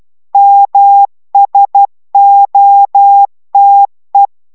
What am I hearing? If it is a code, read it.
Morse code, heard as a keyed tone, about 12 wpm, 800 Hz. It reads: MSOTE